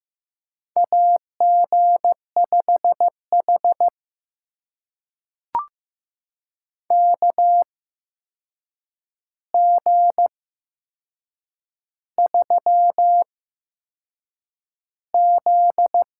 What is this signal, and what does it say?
Morse code, heard as a keyed tone, about 15 wpm, spaced as for 6 wpm, 700 Hz. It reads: AG5H KG3Z
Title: USA Call Signs - 4 Characters - 8x Character Spacing - 15wpm